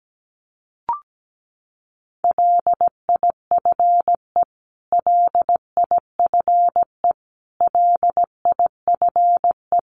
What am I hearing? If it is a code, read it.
Morse code, heard as a keyed tone, about 17 wpm, 700 Hz. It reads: LIFE LIFE LIFE
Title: Top 500 Words Repeated 17wpm